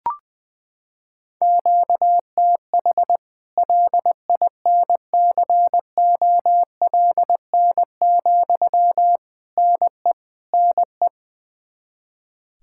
Morse code, heard as a keyed tone, about 20 wpm, 700 Hz. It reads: QTH LINCOLN, NE NE
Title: Practice QSO - Exchange 1 - Beginning - 20wpm